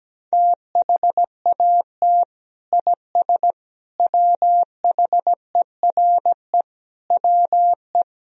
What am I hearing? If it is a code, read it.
Morse code, heard as a keyed tone, about 17 wpm, 700 Hz. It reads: THAT IS WHERE WE
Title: Sentences from Top 1000 Words - Review Part 1 - 17wpm